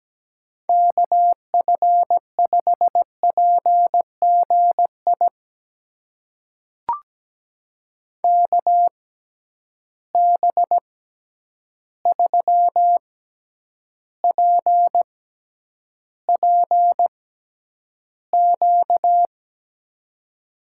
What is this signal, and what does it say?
Morse code, heard as a keyed tone, about 17 wpm, spaced as for 9 wpm, 700 Hz. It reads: KF5PGI KB3PPQ
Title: USA Call Signs - 6 Characters - 6x Character Spacing - 17wpm